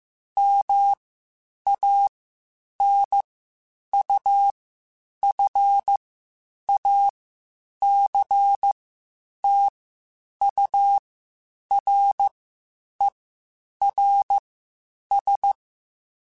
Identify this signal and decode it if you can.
Morse code, heard as a keyed tone, about 15 wpm, spaced as for 8 wpm, 800 Hz. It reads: MANUFACTURERS
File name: Quotes Spaced / Quotes Spaced-mp3s / Airtight sp @13